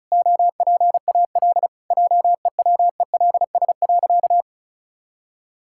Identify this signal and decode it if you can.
Morse code, heard as a keyed tone, about 36 wpm, 700 Hz. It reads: OPAL JEWELS.
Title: Pangram Challenge v2 - 35wpm